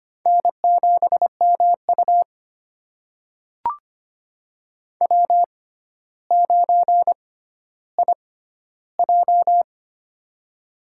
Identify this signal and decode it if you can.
Morse code, heard as a keyed tone, about 25 wpm, spaced as for 13 wpm, 700 Hz. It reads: N7MU W9IJ
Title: USA Call Signs - 4 Characters - 6x Character Spacing - 25wpm